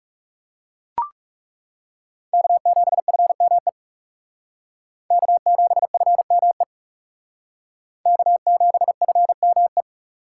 Morse code, heard as a keyed tone, about 36 wpm, 700 Hz. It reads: K7FME K7FME K7FME
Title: Speed Racing - USA Call Signs 30wpm